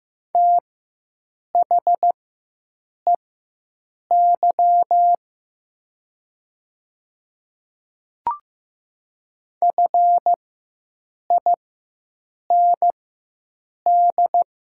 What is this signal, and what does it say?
Morse code, heard as a keyed tone, about 15 wpm, spaced as for 7 wpm, 700 Hz. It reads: THEY FIND
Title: 4x Character Spacing - Top 100 Words - 15wpm